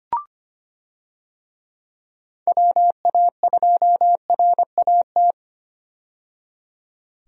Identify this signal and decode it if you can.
Morse code, heard as a keyed tone, about 25 wpm, 700 Hz. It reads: WA2RAT
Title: USA Call Signs - 6 Characters - 25wpm